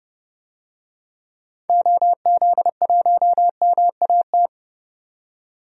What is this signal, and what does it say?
Morse code, heard as a keyed tone, about 30 wpm, 700 Hz. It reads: OZ1MAT
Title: DXCC-100 Call Signs - 6 Characters - 30wpm